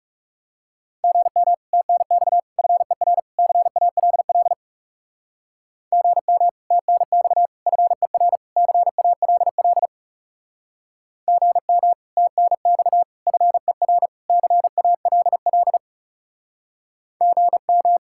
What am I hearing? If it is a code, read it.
Morse code, heard as a keyed tone, about 38 wpm, 700 Hz. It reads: GM TNX FER CALL GM TNX FER CALL GM TNX FER CALL GM